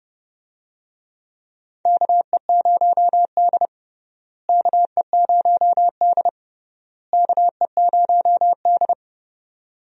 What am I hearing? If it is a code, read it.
Morse code, heard as a keyed tone, about 32 wpm, 700 Hz. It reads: KE0D KE0D KE0D
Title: USA Call Signs - 4 Characters - Repeated 3x - 30wpm